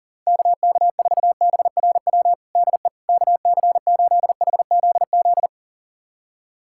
Morse code, heard as a keyed tone, about 40 wpm, 700 Hz. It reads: KK4BRW DE KC8HZZ
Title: USA Callsign DE Callsign - 6 Characters - 40wpm